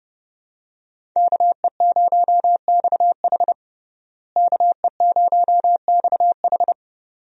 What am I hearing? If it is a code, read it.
Morse code, heard as a keyed tone, about 30 wpm, 700 Hz. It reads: KE0XH KE0XH
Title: USA Call Signs - 5 Characters - Repeated 2x - 30wpm